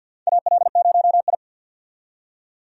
Morse code, heard as a keyed tone, about 48 wpm, 700 Hz. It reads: AL0I
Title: USA Call Signs - 4 Characters - 50wpm